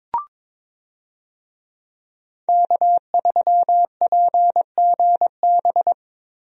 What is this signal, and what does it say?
Morse code, heard as a keyed tone, about 22 wpm, 700 Hz. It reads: K3PGB
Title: USA Call Signs - 5 Characters - 22wpm